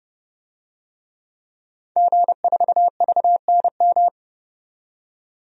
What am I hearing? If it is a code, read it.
Morse code, heard as a keyed tone, about 30 wpm, 700 Hz. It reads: G4VNM
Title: DXCC-100 Call Signs - 5 Characters - 30wpm